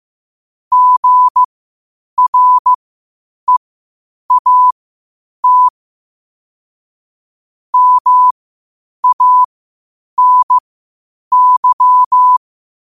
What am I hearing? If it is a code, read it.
Morse code, heard as a keyed tone, about 15 wpm, spaced as for 8 wpm, 1000 Hz. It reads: GREAT MANY
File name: qotdfiles / 8wpm